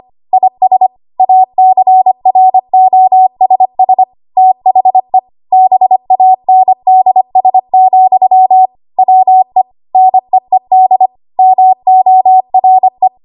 Morse code, heard as a keyed tone, about 25 wpm, 750 Hz. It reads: IS ACROSS THE BANDS, WE NEED MORE
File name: Archive / 25 WPM / 25WPM